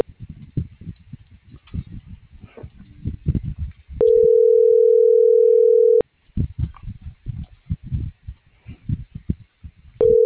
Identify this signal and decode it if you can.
Ringback tone (North America), 440 + 480 Hz.